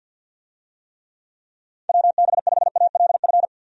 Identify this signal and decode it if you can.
Morse code, heard as a keyed tone, about 48 wpm, 700 Hz. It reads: WB5ALF
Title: USA Call Signs - 6 Characters - 50wpm